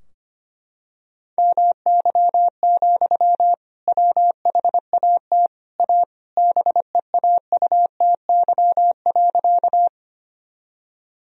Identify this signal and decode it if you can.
Morse code, heard as a keyed tone, about 25 wpm, 700 Hz. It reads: MY, WHAT A BEAUTY.
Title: Sentences from Top 1000 Words - Review Part 2 - 25wpm